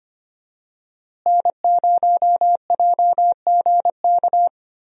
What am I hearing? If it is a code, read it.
Morse code, heard as a keyed tone, about 25 wpm, 700 Hz. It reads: N0JGK
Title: USA Call Signs - 5 Characters - Repeated 2x - 25wpm